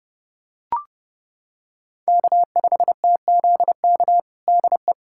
Morse code, heard as a keyed tone, about 32 wpm, 700 Hz. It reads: K5TZK DE
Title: USA Callsign DE Callsign - 5 Characters - 30wpm